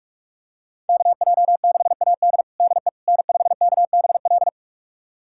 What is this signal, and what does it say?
Morse code, heard as a keyed tone, about 45 wpm, 700 Hz. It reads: KJ6AD DE N5KBL
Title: USA Callsign DE Callsign - 5 Characters - 45wpm